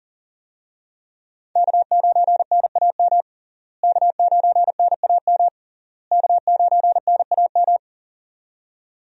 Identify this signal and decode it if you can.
Morse code, heard as a keyed tone, about 40 wpm, 700 Hz. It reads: K9NAM K9NAM K9NAM
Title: USA Call Signs - 5 Characters - Repeated 3x - 40wpm